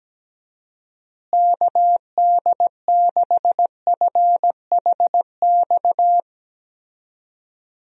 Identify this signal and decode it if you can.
Morse code, heard as a keyed tone, about 17 wpm, 700 Hz. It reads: KD6FHX
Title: USA Call Signs - 6 Characters - Repeated 2x - 17wpm